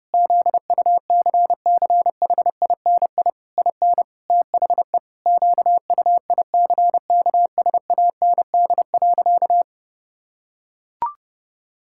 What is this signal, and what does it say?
Morse code, heard as a keyed tone, about 28 wpm, 700 Hz. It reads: ZUCCHINI IN THE QUICKSAND.
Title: Pangram Challenge v2 - 30wpm